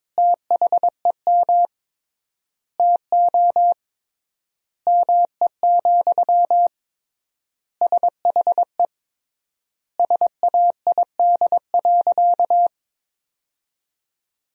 Morse code, heard as a keyed tone, about 21 wpm, 700 Hz. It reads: THEM TO ME, SHE SAID.
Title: 3x Word-Spacing - Sentences from Top 500 Words - 22wpm